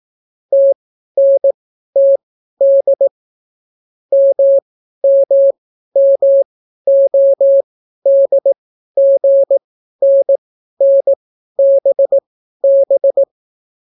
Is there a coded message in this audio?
Yes: Morse code from 0.5 to 13.2 s, heard as a keyed tone, about 18 wpm, spaced as for 12 wpm, 550 Hz. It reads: TNTD MMMODGNNBB